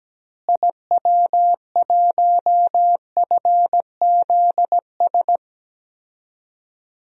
Morse code, heard as a keyed tone, about 17 wpm, 700 Hz. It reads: IW1FZS
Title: DXCC-100 Call Signs - 6 Characters - Repeated 2x - 17wpm